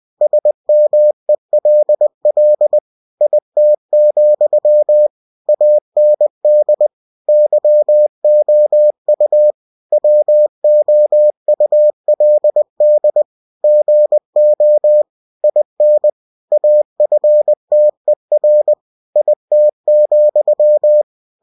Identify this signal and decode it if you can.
Morse code, heard as a keyed tone, about 20 wpm, 600 Hz. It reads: SMELL IT, AND YOU WOULD GO IN AFTER IT,
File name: cw-books / winnie-the-pooh / winnie-20wpm-page-0036